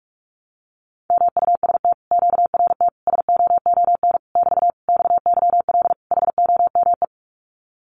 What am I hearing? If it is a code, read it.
Morse code, heard as a keyed tone, about 46 wpm, 700 Hz. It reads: MUST QRT SOON <BT> XYL HOME
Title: Common QSO Phrases Encore 45wpm